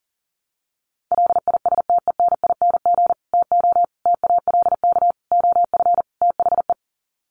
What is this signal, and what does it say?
Morse code, heard as a keyed tone, about 40 wpm, 700 Hz. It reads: LISTENING TO TALK OF THE
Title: Sets of 5 Words 40wpm